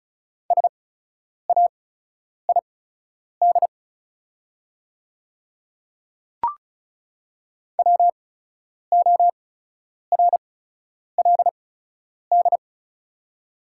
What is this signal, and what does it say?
Morse code, heard as a keyed tone, about 36 wpm, spaced as for 10 wpm, 700 Hz. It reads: SAID WORLD
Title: 8x Character Spacing - Top 200 Words - 35wpm